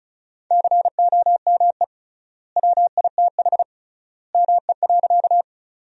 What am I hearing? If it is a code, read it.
Morse code, heard as a keyed tone, about 34 wpm, 700 Hz. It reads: COME WITH ME.
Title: 3x Word-Spacing - Sentences from Top 200 Words - 35wpm